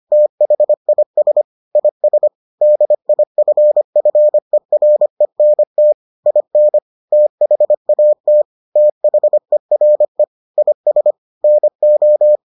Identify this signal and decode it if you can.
Morse code, heard as a keyed tone, about 25 wpm, 600 Hz. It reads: THIS IS DIFFERENT IN THAT THERE IS NO